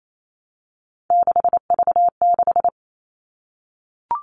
Morse code, heard as a keyed tone, about 28 wpm, 700 Hz. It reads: 6V6
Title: All Cut Numbers - 3-Digit 28wpm